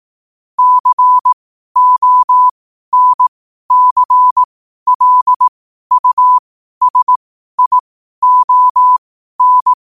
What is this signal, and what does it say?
Morse code, heard as a keyed tone, about 18 wpm, spaced as for 12 wpm, 1000 Hz. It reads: CONCLUSION
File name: qotdfiles / 12wpm